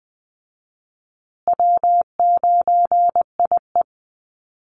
Morse code, heard as a keyed tone, about 20 wpm, 700 Hz. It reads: W9IE
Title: Easy Call Signs Repeated 3x - 20wpm